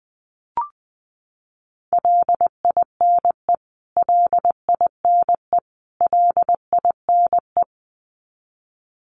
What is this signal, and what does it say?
Morse code, heard as a keyed tone, about 20 wpm, 700 Hz. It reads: LINE LINE LINE
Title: Top 300 Words Repeated 20wpm